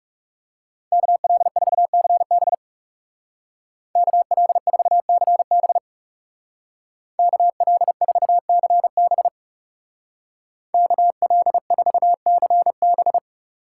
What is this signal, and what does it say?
Morse code, heard as a keyed tone, about 36 wpm, 700 Hz. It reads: KL4CB KL4CB KL4CB KL4CB
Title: Speed Racing - USA Call Signs 30wpm